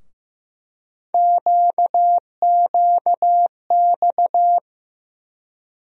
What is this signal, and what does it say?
Morse code, heard as a keyed tone, about 15 wpm, 700 Hz. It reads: QQX
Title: USA Call Sign Postfixes - 3 Letters 15wpm